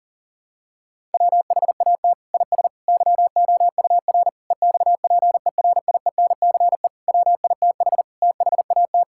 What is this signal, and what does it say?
Morse code, heard as a keyed tone, about 37 wpm, 700 Hz. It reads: WHAT IS YOUR EXPERIENCE WITH THAT
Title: Sentences from Top 1000 Words - Review Part 5 - 40wpm